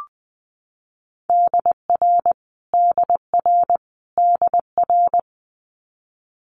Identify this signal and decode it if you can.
Morse code, heard as a keyed tone, about 20 wpm, 700 Hz. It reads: DR DR DR
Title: Multi-letter Phonemes Repeated 3x 20wpm